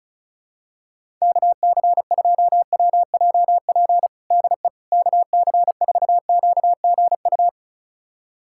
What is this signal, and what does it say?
Morse code, heard as a keyed tone, about 34 wpm, 700 Hz. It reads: KC2WJP DE KC4QGU
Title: USA Callsign DE Callsign - 6 Characters - 35wpm